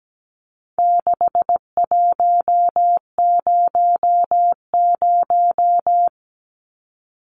Morse code, heard as a keyed tone, about 17 wpm, 700 Hz. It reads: 6100
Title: Numbers - 4-Digit 17wpm